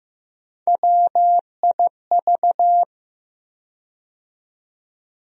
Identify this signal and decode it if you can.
Morse code, heard as a keyed tone, about 15 wpm, 700 Hz. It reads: WIV
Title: USA Call Sign Postfixes - 3 Letters 15wpm